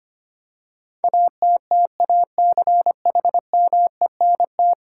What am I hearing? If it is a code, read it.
Morse code, heard as a keyed tone, about 25 wpm, 700 Hz. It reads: ATTACHMENT
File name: Ten Letter Words - 25wpm